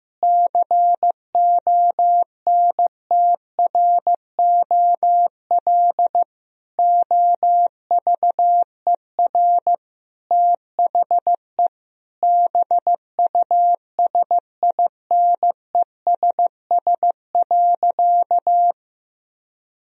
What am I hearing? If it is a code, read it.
Morse code, heard as a keyed tone, about 15 wpm, 700 Hz. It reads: CONTROL OVER THE BUSINESS.
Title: Sentences from Top 1000 Words - Review Part 3 - 15wpm